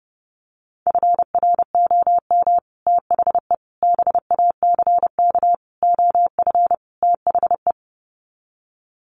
Morse code, heard as a keyed tone, about 30 wpm, 700 Hz. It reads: FROM THE BACK OF THE
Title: Sets of 5 Words Encore 30wpm